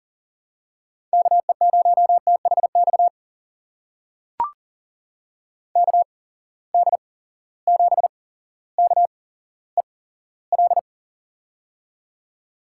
Morse code, heard as a keyed tone, about 38 wpm, spaced as for 17 wpm, 700 Hz. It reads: KE0THX KD7KEL
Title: USA Call Signs - 6 Characters - 8x Character Spacing - 40wpm